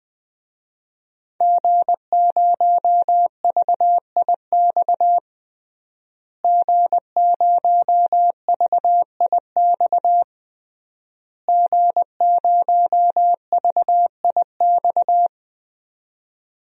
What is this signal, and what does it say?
Morse code, heard as a keyed tone, about 20 wpm, 700 Hz. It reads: G0VIX G0VIX G0VIX
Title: DXCC-100 Call Signs - 5 Characters - Repeated 3x - 20wpm